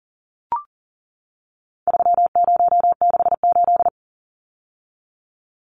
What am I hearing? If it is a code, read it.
Morse code, heard as a keyed tone, about 40 wpm, 700 Hz. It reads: 3068
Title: Numbers - 4-Digit 40wpm